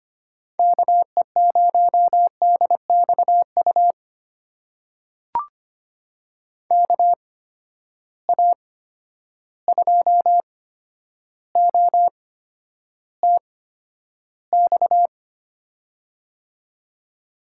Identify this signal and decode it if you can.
Morse code, heard as a keyed tone, about 25 wpm, spaced as for 11 wpm, 700 Hz. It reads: KE0DXU KA2OTX
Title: USA Call Signs - 6 Characters - 8x Character Spacing - 25wpm